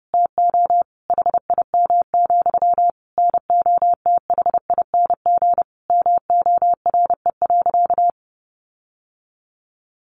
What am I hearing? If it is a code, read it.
Morse code, heard as a keyed tone, about 30 wpm, 700 Hz. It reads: TO HIM, NOTHING MORE.
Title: Sentences from Top 1300 Words 30wpm